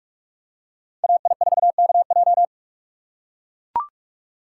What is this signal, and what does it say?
Morse code, heard as a keyed tone, about 44 wpm, 700 Hz. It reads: AI4KJ
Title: USA Call Signs - 5 Characters - 6x Character Spacing - 45wpm